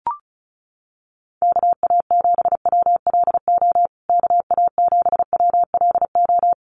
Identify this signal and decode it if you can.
Morse code, heard as a keyed tone, about 34 wpm, 700 Hz. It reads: KA7WLO KA7WLO
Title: USA Call Signs Repeated 2x 35wpm